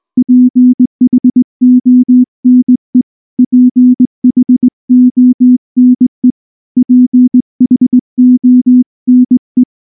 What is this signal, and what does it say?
Morse code, heard as a keyed tone, about 19 wpm, 250 Hz. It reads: PHONE PHONE PHONE
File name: morse / morse-training-20WPM-8